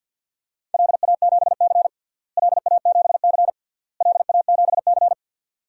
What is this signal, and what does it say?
Morse code, heard as a keyed tone, about 51 wpm, 700 Hz. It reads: LA7C LA7C LA7C
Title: DXCC-100 Call Signs - 4 Characters - Repeated 3x - 50wpm